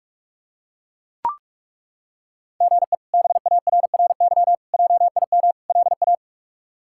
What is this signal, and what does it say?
Morse code, heard as a keyed tone, about 42 wpm, 700 Hz. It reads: GE BARRY JIM LA
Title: K1USN SST QSO Practice - 45wpm